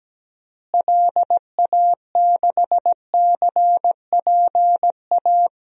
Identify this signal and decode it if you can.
Morse code, heard as a keyed tone, about 17 wpm, 700 Hz. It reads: LA6CPA
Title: DXCC-100 Call Signs - 6 Characters - Repeated 2x - 17wpm